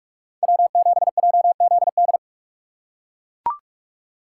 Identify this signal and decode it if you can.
Morse code, heard as a keyed tone, about 44 wpm, 700 Hz. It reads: W7JZD
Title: USA Call Signs - 5 Characters - 6x Character Spacing - 45wpm